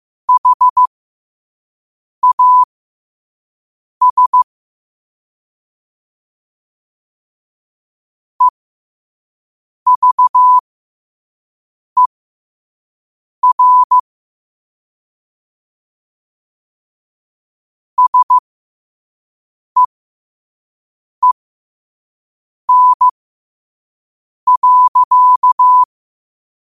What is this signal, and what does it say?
Morse code, heard as a keyed tone, about 15 wpm, spaced as for 5 wpm, 1000 Hz. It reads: HAS EVER SEEN.
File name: qotdfiles / 5wpm